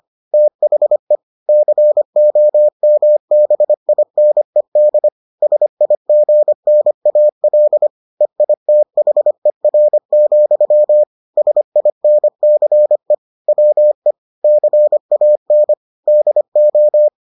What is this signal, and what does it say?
Morse code, heard as a keyed tone, about 25 wpm, 600 Hz. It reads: THE COMBINED SIGNAL EITHER, SINCE WE CAN DO